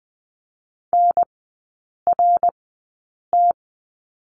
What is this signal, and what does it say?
Morse code, heard as a keyed tone, about 20 wpm, spaced as for 8 wpm, 700 Hz. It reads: NRT
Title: 053 Lesson 10e - Letter r - Three Characters 20wpm